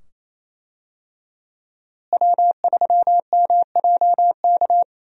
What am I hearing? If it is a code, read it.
Morse code, heard as a keyed tone, about 28 wpm, 700 Hz. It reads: W3MJK
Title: USA Call Signs - 5 Characters - Repeated 2x - 28wpm